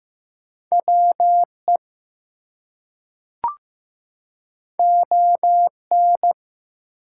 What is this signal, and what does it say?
Morse code, heard as a keyed tone, about 15 wpm, 700 Hz. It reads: WE ON
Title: Two Letter Words - 15wpm